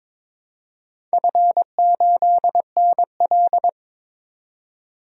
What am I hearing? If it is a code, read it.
Morse code, heard as a keyed tone, about 22 wpm, 700 Hz. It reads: F8NL
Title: DXCC-100 Call Signs - 4 Characters - 22wpm